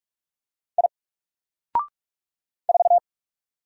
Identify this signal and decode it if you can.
Morse code, heard as a keyed tone, about 46 wpm, 700 Hz. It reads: I 4
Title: Single Letter-Number 45wpm